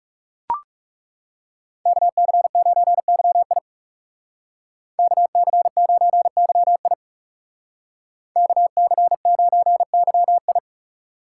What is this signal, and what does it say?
Morse code, heard as a keyed tone, about 39 wpm, 700 Hz. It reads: KC9YI KC9YI KC9YI